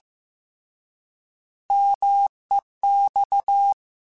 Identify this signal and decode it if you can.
Morse code, heard as a keyed tone, about 15 wpm, 800 Hz. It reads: MEX
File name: Contrasts / Contrasts-mp3s / 3LtrContrasts @13